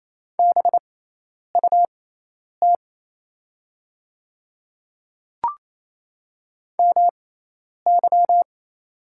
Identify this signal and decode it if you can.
Morse code, heard as a keyed tone, about 27 wpm, spaced as for 10 wpm, 700 Hz. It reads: BUT MY
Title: 6x Character Spacing - Top 200 Words - 28wpm